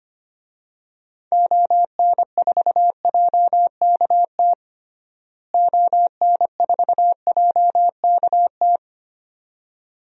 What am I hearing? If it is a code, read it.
Morse code, heard as a keyed tone, about 25 wpm, 700 Hz. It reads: ON4JKT ON4JKT
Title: DXCC-100 Call Signs - 6 Characters - Repeated 2x - 25wpm